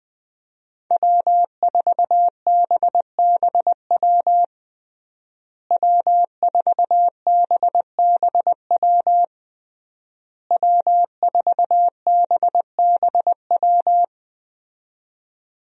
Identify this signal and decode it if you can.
Morse code, heard as a keyed tone, about 20 wpm, 700 Hz. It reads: W4BBW W4BBW W4BBW
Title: USA Call Signs - 5 Characters - Repeated 3x - 20wpm